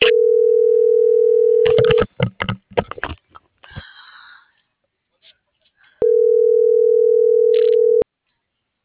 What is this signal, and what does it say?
Ringback tone (North America), 440 + 480 Hz.